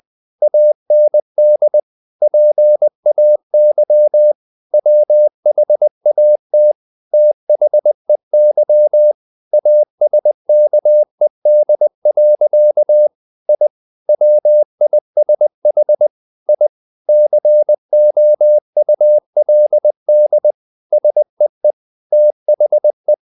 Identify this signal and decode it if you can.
Morse code, heard as a keyed tone, about 20 wpm, 600 Hz. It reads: AND PAY WHAT THEY ASKED. I WISH I COULD SEE THE
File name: the old man and the sea part 2 20wpm